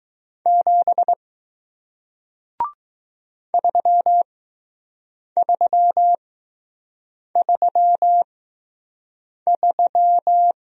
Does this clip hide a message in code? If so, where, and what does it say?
Yes: Morse code from 0.5 to 10.5 s, heard as a keyed tone, about 20 wpm, spaced as for 6 wpm, 700 Hz. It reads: 7 3333